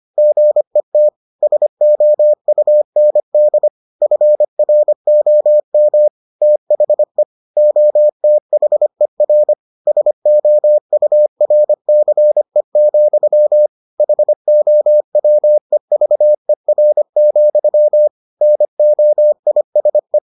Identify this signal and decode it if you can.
Morse code, heard as a keyed tone, about 25 wpm, 600 Hz. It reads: GET SOUND FROM THE OTHER SOURCE, HOWEVER, NOISE